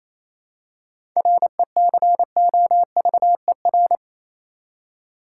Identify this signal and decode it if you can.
Morse code, heard as a keyed tone, about 28 wpm, 700 Hz. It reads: RECOVER
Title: Seven Letter Words - 28wpm